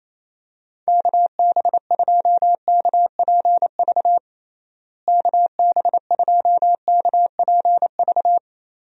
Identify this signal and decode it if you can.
Morse code, heard as a keyed tone, about 27 wpm, 700 Hz. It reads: KB2KPV KB2KPV
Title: USA Call Signs - 6 Characters - Repeated 3x - 28wpm